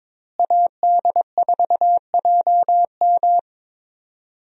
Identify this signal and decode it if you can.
Morse code, heard as a keyed tone, about 22 wpm, 700 Hz. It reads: AD4JM
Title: USA Call Signs - 5 Characters - Repeated 3x - 22wpm